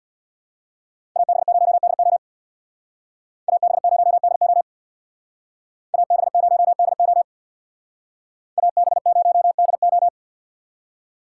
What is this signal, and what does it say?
Morse code, heard as a keyed tone, about 62 wpm, 700 Hz. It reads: AB0DO AB0DO AB0DO AB0DO
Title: Speed Racing - USA 2x2 Call Signs 50wpm